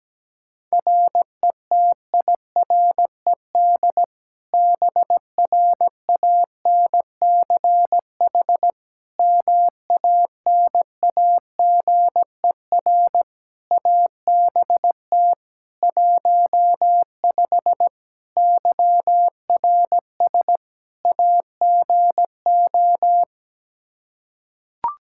Morse code, heard as a keyed tone, about 17 wpm, 700 Hz. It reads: RETIRED BRANCH MANAGER ABT 15 YRS AGO
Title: Practice QSO - Exchange 3 - More Info - 17wpm